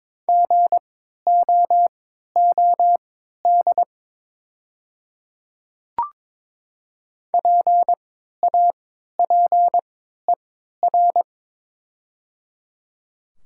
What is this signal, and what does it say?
Morse code, heard as a keyed tone, about 22 wpm, spaced as for 12 wpm, 700 Hz. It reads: GOOD PAPER
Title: 3x Character Spacing - Top 300 Words - 22wpm